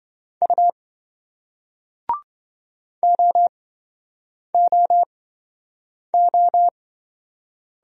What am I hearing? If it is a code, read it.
Morse code, heard as a keyed tone, about 28 wpm, spaced as for 7 wpm, 700 Hz. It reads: U OOO